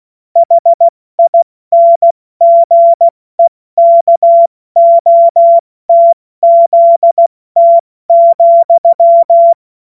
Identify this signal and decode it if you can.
Morse code, heard as a keyed tone, about 16 wpm, 680 Hz. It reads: HINGEKOTZT,
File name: Morsekurs / Witze / Witz 02 Januar 2024 70 F80